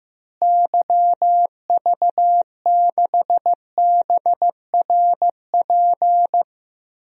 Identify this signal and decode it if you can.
Morse code, heard as a keyed tone, about 16 wpm, 700 Hz. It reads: YV6BRP
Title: DXCC-100 Call Signs - 6 Characters - Repeated 2x - 15wpm